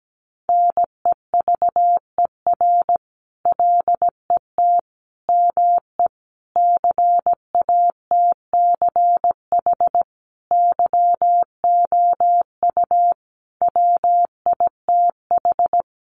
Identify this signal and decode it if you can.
Morse code, heard as a keyed tone, about 17 wpm, 700 Hz. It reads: NEVER LET ME CATCH YOU WITH
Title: Sentences from Top 1500 Words 17wpm